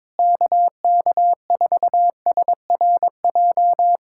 Morse code, heard as a keyed tone, about 22 wpm, 700 Hz. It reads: KK4SRJ
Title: Speed-Racing Common QSO Phrases - 22wpm